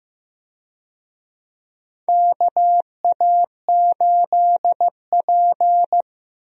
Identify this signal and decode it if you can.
Morse code, heard as a keyed tone, about 15 wpm, 700 Hz. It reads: KA8P
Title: USA Call Signs - 4 Characters - Repeated 2x - 15wpm